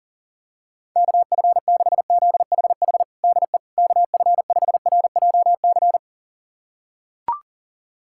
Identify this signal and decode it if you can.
Morse code, heard as a keyed tone, about 40 wpm, 700 Hz. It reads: KF6ZHH DE KF5RJC
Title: USA Callsign DE Callsign 40wpm